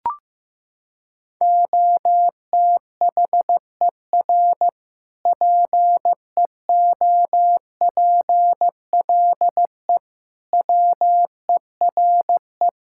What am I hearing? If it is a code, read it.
Morse code, heard as a keyed tone, about 15 wpm, 700 Hz. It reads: OTHER PEOPLE WERE
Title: Sentences from Top 1000 Words - Review Part 5 - 15wpm